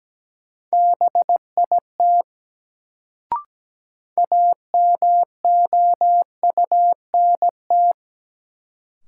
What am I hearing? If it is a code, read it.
Morse code, heard as a keyed tone, about 17 wpm, 700 Hz. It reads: BIT AMOUNT
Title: Sets of 3 Words - New Word Drill - 17wpm